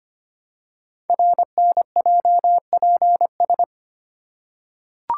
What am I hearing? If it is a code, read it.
Morse code, heard as a keyed tone, about 25 wpm, 700 Hz. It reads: RNJPS
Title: Code Groups - 5 Letters - 25wpm